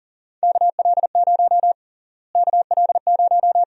Morse code, heard as a keyed tone, about 40 wpm, 700 Hz. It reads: KL0 KL0
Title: USA Call Sign Prefixes - Repeated 2x - 40wpm